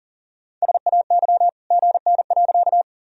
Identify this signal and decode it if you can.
Morse code, heard as a keyed tone, about 37 wpm, 700 Hz. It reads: SAY GN.
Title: Sentences from Top 300 Words and Abbreviations 40wpm